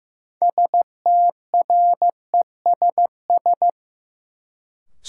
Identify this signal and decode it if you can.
Morse code, heard as a keyed tone, about 15 wpm, 700 Hz. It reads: STRESS
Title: Sentences from Top 1000 Words - Review Part 6 - New Word Drill - 15wpm